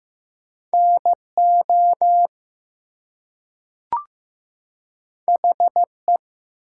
Morse code, heard as a keyed tone, about 15 wpm, 700 Hz. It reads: NO HE
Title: Two Letter Words - 15wpm